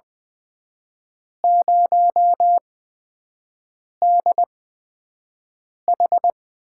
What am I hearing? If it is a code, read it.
Morse code, heard as a keyed tone, about 21 wpm, spaced as for 5 wpm, 700 Hz. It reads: 0DH
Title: USA Call Signs - 6 Characters - 8x Character Spacing - 20wpm